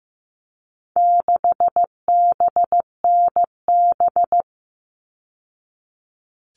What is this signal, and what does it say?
Morse code, heard as a keyed tone, about 15 wpm, 700 Hz. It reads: 6BNB
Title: All Cut Numbers - 4-Digit 15wpm